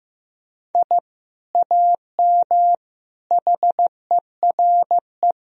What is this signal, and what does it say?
Morse code, heard as a keyed tone, about 15 wpm, 700 Hz. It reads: I AM HERE
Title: Sentences from Top 1000 Words - Review Part 3 - 15wpm